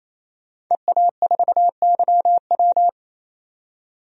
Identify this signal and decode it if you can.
Morse code, heard as a keyed tone, about 28 wpm, 700 Hz. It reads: EA4YW
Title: DXCC-100 Call Signs - 5 Characters - 28wpm